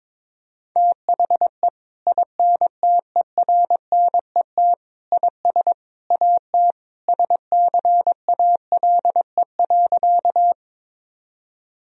Morse code, heard as a keyed tone, about 22 wpm, 700 Hz. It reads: THE INTERNET IS AT SCALE.
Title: Sentences from Top 1000 Words - Review Part 4 - 22wpm